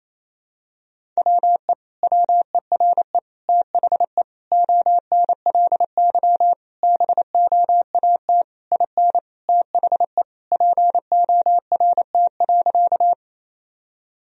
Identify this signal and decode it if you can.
Morse code, heard as a keyed tone, about 28 wpm, 700 Hz. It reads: WE WERE THE ONLY BOAT IN THE PORT.
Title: Sentences from Top 1000 Words - Review Part 1 - 28wpm